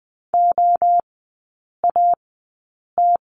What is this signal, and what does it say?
Morse code, heard as a keyed tone, about 20 wpm, spaced as for 8 wpm, 700 Hz. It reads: OAT
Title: 035 Lesson 7e - Letter s - Three Characters 20wpm